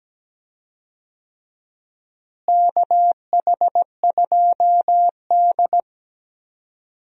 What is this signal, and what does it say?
Morse code, heard as a keyed tone, about 17 wpm, 700 Hz. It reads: KH2D
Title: USA Call Signs - 4 Characters - Repeated 2x - 17wpm